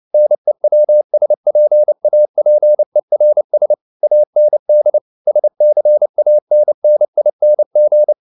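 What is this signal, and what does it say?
Morse code, heard as a keyed tone, about 29 wpm, 600 Hz. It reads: NEWSPAPERS AND SCANNING
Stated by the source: Morse code at 29 wpm, sending Ken Follett's novel Never in English, from CW med Gnister.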